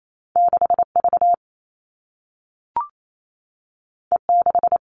Morse code, heard as a keyed tone, about 28 wpm, 700 Hz. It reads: 6V E6
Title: All Cut Numbers - 2-Digit 28wpm